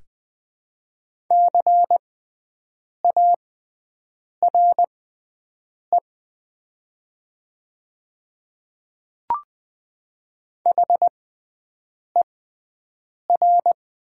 Morse code, heard as a keyed tone, about 20 wpm, spaced as for 7 wpm, 700 Hz. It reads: CARE HER
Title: 6x Character Spacing - Top 300 Words - 20wpm